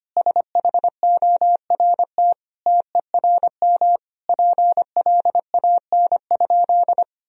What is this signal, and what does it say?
Morse code, heard as a keyed tone, about 25 wpm, 700 Hz. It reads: SHORT TERM PLAN?
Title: Sentences from Top 1000 Words - Review Part 4 - 25wpm